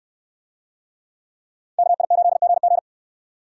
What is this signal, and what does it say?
Morse code, heard as a keyed tone, about 64 wpm, 700 Hz. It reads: KI8GG